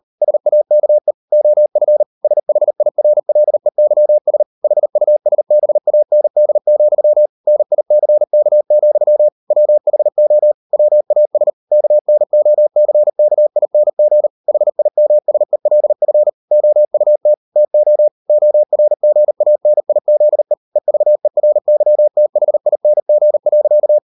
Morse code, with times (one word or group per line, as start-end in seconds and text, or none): 0.2-1.1 SAKE
1.3-2.0 OF
2.2-4.4 SHIRLEYS
4.6-7.3 HUSBAND,
7.5-9.3 NICK,
9.5-10.5 WHO
10.7-11.5 WAS
11.7-14.3 KNOCKING
14.5-16.3 HIMSELF
16.5-17.3 OUT
17.6-18.1 TO
18.3-20.5 ORGANIZE
20.8-24.0 EVERYTHING.